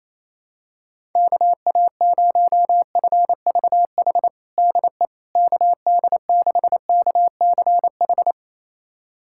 Morse code, heard as a keyed tone, about 28 wpm, 700 Hz. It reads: KA0FVH DE KD6KCH
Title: USA Callsign DE Callsign - 6 Characters - 28wpm